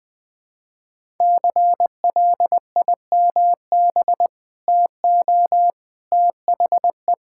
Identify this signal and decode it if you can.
Morse code, heard as a keyed tone, about 21 wpm, 700 Hz. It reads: CLIMB TO THE
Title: Sentences with 4-Word Phrases - 20wpm